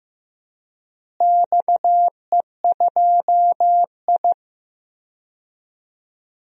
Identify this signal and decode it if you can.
Morse code, heard as a keyed tone, about 15 wpm, 700 Hz. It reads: XE2I
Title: DXCC-100 Call Signs - 4 Characters - Repeated 3x - 15wpm